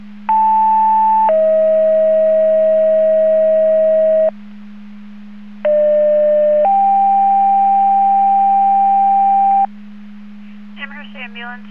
Two-tone page, heard as a one-second tone, then a three-second tone, about 880 then 640 Hz, about 600 then 790 Hz. It is heard from 0.3 to 9.7 s.
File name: Portage / 00947 12-49 Portage 154.385 Paging